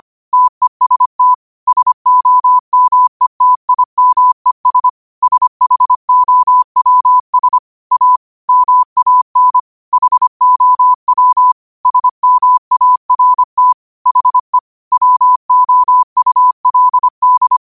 Morse code, heard as a keyed tone, about 25 wpm, 1000 Hz. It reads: TEST SOMETIMES SHOWS A MAN HOW SMART HE WOULD